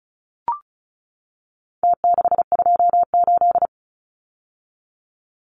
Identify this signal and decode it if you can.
Morse code, heard as a keyed tone, about 34 wpm, 700 Hz. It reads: T628
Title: Numbers - Contest Style - 4-Digit 35wpm